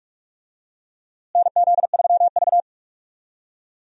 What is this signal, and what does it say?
Morse code, heard as a keyed tone, about 46 wpm, 700 Hz. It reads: NZ3V
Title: USA Call Signs - 4 Characters - Repeated 3x - 45wpm